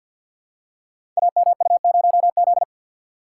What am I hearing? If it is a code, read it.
Morse code, heard as a keyed tone, about 50 wpm, 700 Hz. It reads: AMU0Z
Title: Code Groups - 5 Letter:Numbers - 50wpm